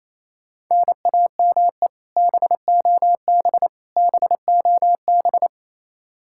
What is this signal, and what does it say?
Morse code, heard as a keyed tone, about 28 wpm, 700 Hz. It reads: NAME BOB BOB
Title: Practice QSO - Exchange 1 - Beginning - 28wpm